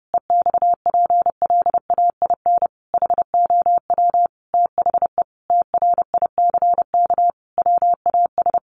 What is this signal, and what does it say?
Morse code, heard as a keyed tone, about 28 wpm, 700 Hz. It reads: EXPLAIN HOW THE TRICK WAS
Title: Sentences from Top 1500 Words 30wpm